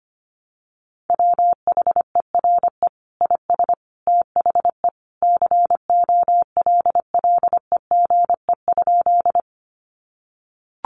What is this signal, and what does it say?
Morse code, heard as a keyed tone, about 25 wpm, 700 Hz. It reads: WHERE IS THE COLLEGE?
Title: Sentences from Top 1300 Words 25wpm